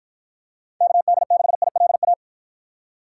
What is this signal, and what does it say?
Morse code, heard as a keyed tone, about 52 wpm, 700 Hz. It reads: KD6ILA